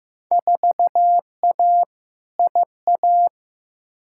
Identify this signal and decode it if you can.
Morse code, heard as a keyed tone, about 15 wpm, 700 Hz. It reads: 4A IA
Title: Field Day Exchanges - 15wpm